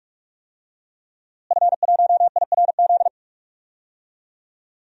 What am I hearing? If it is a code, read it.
Morse code, heard as a keyed tone, about 44 wpm, 700 Hz. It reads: F1IRZ
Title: DXCC-100 Call Signs - 5 Characters - Repeated 3x - 45wpm